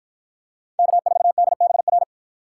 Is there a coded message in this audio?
Yes: Morse code from 0.8 to 2.0 s, heard as a keyed tone, about 53 wpm, 700 Hz. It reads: K4DBR